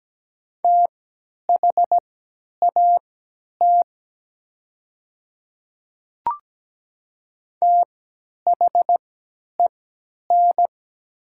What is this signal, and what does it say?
Morse code, heard as a keyed tone, about 17 wpm, spaced as for 10 wpm, 700 Hz. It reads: THAT THEN
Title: 3x Character Spacing - Top 100 Words - 17wpm